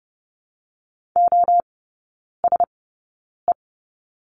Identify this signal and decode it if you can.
Morse code, heard as a keyed tone, about 30 wpm, spaced as for 9 wpm, 700 Hz. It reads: OSE
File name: Lesson 16e - Letter c - Three Characters 30wpm